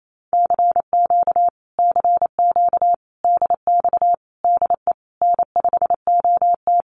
Morse code, heard as a keyed tone, about 28 wpm, 700 Hz. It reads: CQ CQ DX DE N5OT
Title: Common QSO Phrases Encore 28wpm